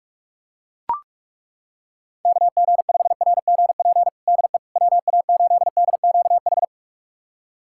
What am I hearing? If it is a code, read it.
Morse code, heard as a keyed tone, about 41 wpm, 690 Hz. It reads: KG5RGP DE WA8DQH
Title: USA Callsign DE Callsign 45wpm